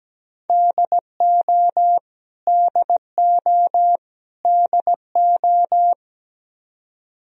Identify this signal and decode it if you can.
Morse code, heard as a keyed tone, about 17 wpm, 700 Hz. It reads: DO DO DO
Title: Top 100 Words Repeated 17wpm